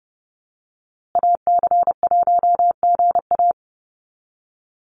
Morse code, heard as a keyed tone, about 30 wpm, 700 Hz. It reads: AC1GA
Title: USA Call Signs Repeated 2x 30wpm